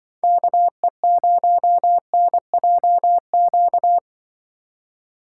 Morse code, heard as a keyed tone, about 23 wpm, 700 Hz. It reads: KE0NJQ
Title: Speed Racing - USA 2x3 Call Signs 20wpm